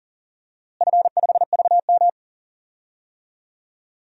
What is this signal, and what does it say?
Morse code, heard as a keyed tone, about 39 wpm, 700 Hz. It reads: F5VM
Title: DXCC-100 Call Signs - 4 Characters - Repeated 2x - 40wpm